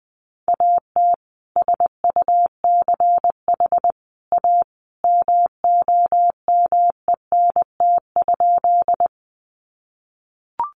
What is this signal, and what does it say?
Morse code, heard as a keyed tone, about 20 wpm, 700 Hz. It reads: AT SUCH A MOMENT?
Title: Sentences from Top 1000 Words 20wpm